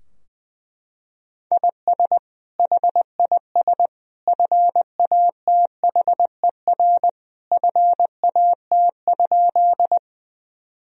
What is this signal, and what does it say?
Morse code, heard as a keyed tone, about 20 wpm, 700 Hz. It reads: IS HIS FATHER FAT?
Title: Sentences from Top 1000 Words - Review Part 4 - 20wpm